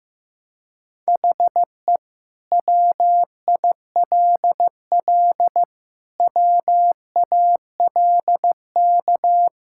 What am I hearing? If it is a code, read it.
Morse code, heard as a keyed tone, about 16 wpm, 700 Hz. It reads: HE WILL WALK
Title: Sentences from Top 1600 Words 15wpm